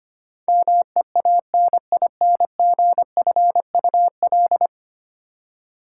Morse code, heard as a keyed tone, about 25 wpm, 700 Hz. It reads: MEANINGFUL
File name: Ten Letter Words - 25wpm